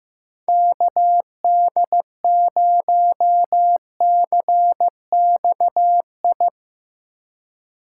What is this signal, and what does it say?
Morse code, heard as a keyed tone, about 16 wpm, 700 Hz. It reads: KD0CXI
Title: USA Call Signs - 6 Characters - 2x Character Spacing - 15wpm